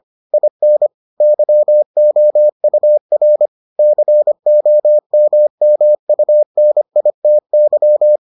Morse code, heard as a keyed tone, about 25 wpm, 600 Hz. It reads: IN YOUR COMMUNITY